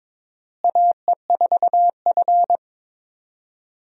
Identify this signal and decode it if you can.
Morse code, heard as a keyed tone, about 22 wpm, 700 Hz. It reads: AE4F
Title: USA Call Signs - 4 Characters - Repeated 3x - 22wpm